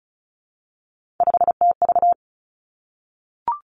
Morse code, heard as a keyed tone, about 36 wpm, 700 Hz. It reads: 5TV
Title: Three Challenging Letter-Number Combinations 35wpm